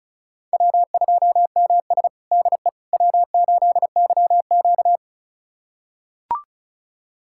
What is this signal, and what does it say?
Morse code, heard as a keyed tone, about 34 wpm, 700 Hz. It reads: W2MS DE W8YQ
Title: USA Callsign DE Callsign - 4 Characters - 35wpm